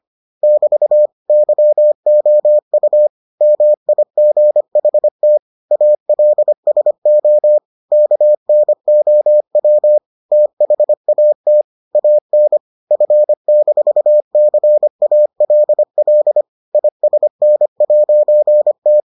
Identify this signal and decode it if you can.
Morse code, heard as a keyed tone, about 25 wpm, 600 Hz. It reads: <BT> YOU MIGHT ALSO KNOW THAT AN F-CALL ISN'T